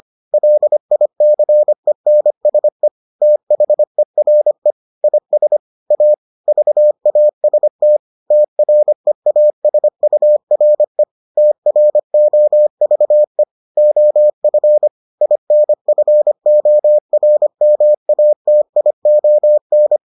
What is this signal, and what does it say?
Morse code, heard as a keyed tone, about 25 wpm, 600 Hz. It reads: LICENSE THERE IS A VAST TREASURE TROVE OF INFORMATION